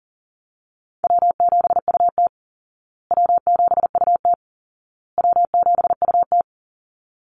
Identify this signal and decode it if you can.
Morse code, heard as a keyed tone, about 40 wpm, 700 Hz. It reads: W7UT W7UT W7UT
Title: Call Signs Repeated 2x - 40wpm